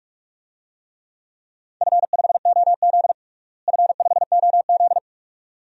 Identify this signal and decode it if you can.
Morse code, heard as a keyed tone, about 46 wpm, 700 Hz. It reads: F5OZ F5OZ
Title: DXCC-100 Call Signs - 4 Characters - Repeated 2x - 45wpm